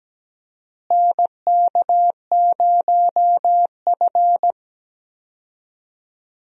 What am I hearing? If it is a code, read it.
Morse code, heard as a keyed tone, about 17 wpm, 700 Hz. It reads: NK0F
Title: USA Call Signs - 4 Characters - Repeated 3x - 17wpm